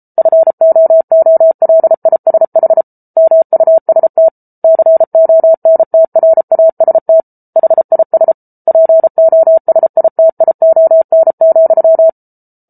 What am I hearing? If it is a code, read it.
Morse code, heard as a keyed tone, about 34 wpm, 650 Hz. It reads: FOOLISH MUST CONTRAST HIS POSITION,